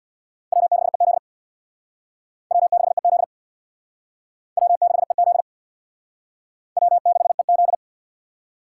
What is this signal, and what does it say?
Morse code, heard as a keyed tone, about 61 wpm, 700 Hz. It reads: W6EZ W6EZ W6EZ W6EZ
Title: Speed Racing - USA 1x2 Call Signs 50wpm